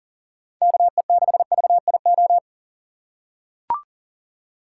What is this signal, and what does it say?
Morse code, heard as a keyed tone, about 37 wpm, 700 Hz. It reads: KE6VIO
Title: USA Call Signs - 4x Character Spacing - 40wpm